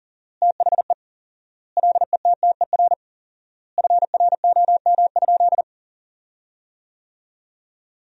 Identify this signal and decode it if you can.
Morse code, heard as a keyed tone, about 40 wpm, 700 Hz. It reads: THE LETTER FROM?
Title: 4x Word-Spacing - Sentences from Top 300 Words - 40wpm